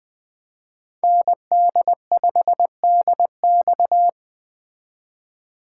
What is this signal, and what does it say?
Morse code, heard as a keyed tone, about 20 wpm, 700 Hz. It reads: ND5DX
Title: Speed Racing - USA 2x2 Call Signs 17wpm